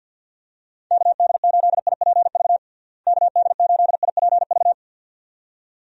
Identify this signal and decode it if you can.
Morse code, heard as a keyed tone, about 52 wpm, 700 Hz. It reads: KD8IPV KD8IPV
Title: USA Call Signs - 6 Characters - Repeated 2x - 50wpm